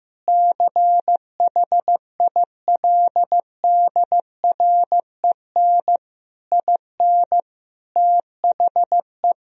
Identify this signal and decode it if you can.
Morse code, heard as a keyed tone, about 15 wpm, 700 Hz. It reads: CHILDREN IN THE
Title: Sentences from Top 1000 Words - Review Part 4 - 15wpm